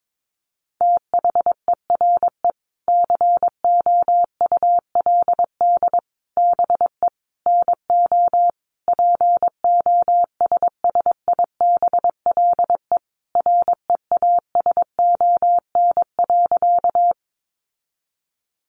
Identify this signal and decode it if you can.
Morse code, heard as a keyed tone, about 22 wpm, 700 Hz. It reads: THERE COULD BE NO POSSIBLE REASON.
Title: Sentences from Top 700 Words 22wpm